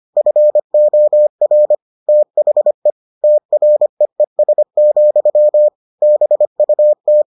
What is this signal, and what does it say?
Morse code, heard as a keyed tone, about 25 wpm, 600 Hz. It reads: FOR THE TREES, BUT